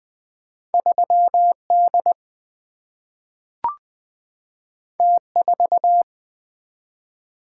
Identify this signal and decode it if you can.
Morse code, heard as a keyed tone, about 20 wpm, 700 Hz. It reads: 3 D T 4
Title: Single-Letter Single-Number 20wpm